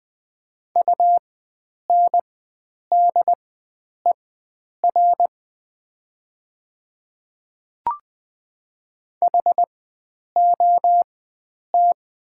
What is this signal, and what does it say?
Morse code, heard as a keyed tone, about 21 wpm, spaced as for 9 wpm, 700 Hz. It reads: UNDER HOT
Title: 4x Character Spacing - Top 500 Words - 20wpm